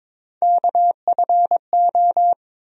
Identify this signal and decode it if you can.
Morse code, heard as a keyed tone, about 22 wpm, 700 Hz. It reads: KFO
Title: USA Call Sign Postfixes - 3 Letters 22wpm